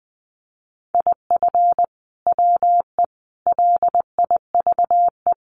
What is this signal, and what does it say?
Morse code, heard as a keyed tone, about 20 wpm, 700 Hz. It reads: IF WE LIVE
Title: Sentences from Top 1400 Words 20wpm